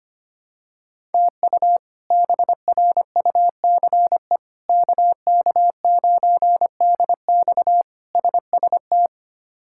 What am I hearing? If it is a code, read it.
Morse code, heard as a keyed tone, about 25 wpm, 700 Hz. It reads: TU BRUCE KK9DX SST
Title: K1USN SST QSO Practice - 25wpm